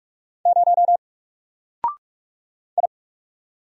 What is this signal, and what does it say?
Morse code, heard as a keyed tone, about 44 wpm, 700 Hz. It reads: 0 I
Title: Single Character (Letter:Number:Punctuation:Prosign) Alt 45wpm